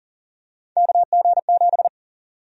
Morse code, heard as a keyed tone, about 40 wpm, 700 Hz. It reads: KG7
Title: USA Call Sign Prefixes - Repeated 2x - 40wpm